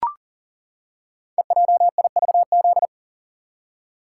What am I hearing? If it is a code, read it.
Morse code, heard as a keyed tone, about 39 wpm, 700 Hz. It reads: EJIVZ
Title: Code Groups - 5 Letters - 40wpm